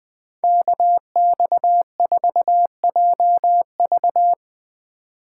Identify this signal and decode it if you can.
Morse code, heard as a keyed tone, about 20 wpm, 700 Hz. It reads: KX4JV